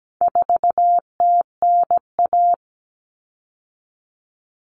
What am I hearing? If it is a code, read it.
Morse code, heard as a keyed tone, about 17 wpm, 700 Hz. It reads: 4TNA
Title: Common Cut Numbers - 4-Digit 17wpm